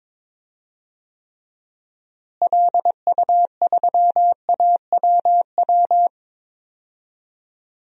Morse code, heard as a keyed tone, about 22 wpm, 700 Hz. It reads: LU3AWW
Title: DXCC-100 Call Signs - 6 Characters - Repeated 2x - 22wpm